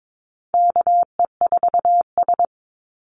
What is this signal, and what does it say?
Morse code, heard as a keyed tone, about 22 wpm, 690 Hz. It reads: KE4S
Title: Call Signs Repeated 3x - 22wpm